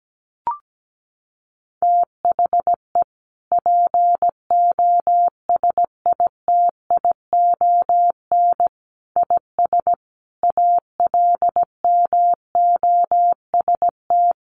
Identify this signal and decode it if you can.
Morse code, heard as a keyed tone, about 17 wpm, 700 Hz. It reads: THE POSITION IS ALMOST
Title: Sentences from Top 1300 Words 17wpm